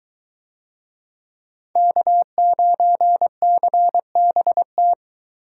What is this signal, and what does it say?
Morse code, heard as a keyed tone, about 22 wpm, 700 Hz. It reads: K9CBT